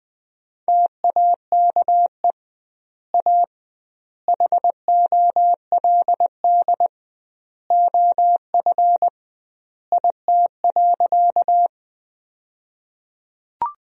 Morse code, heard as a keyed tone, about 20 wpm, 700 Hz. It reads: TAKE A HOLD OF IT.
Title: 2x Word-Spacing - Sentences from Top 400 Words - 20wpm